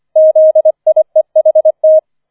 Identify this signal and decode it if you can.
Morse code, heard as a keyed tone, about 25 wpm, 630 Hz. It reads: ZIEHT
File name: 004 Vierter Streich 25wpm